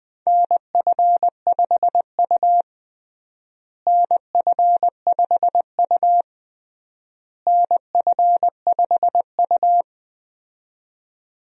Morse code, heard as a keyed tone, about 20 wpm, 700 Hz. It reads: NF5U NF5U NF5U
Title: USA Call Signs - 4 Characters - Repeated 3x - 20wpm